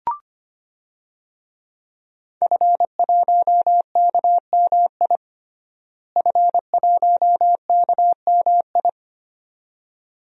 Morse code, heard as a keyed tone, about 25 wpm, 700 Hz. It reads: F1KMI F1KMI
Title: DXCC-100 Call Signs - 5 Characters - Repeated 2x - 25wpm